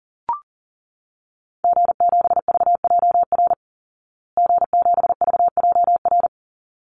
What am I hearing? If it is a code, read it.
Morse code, heard as a keyed tone, about 40 wpm, 700 Hz. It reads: G7VJR G7VJR
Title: Call Signs Repeated 3x - 40wpm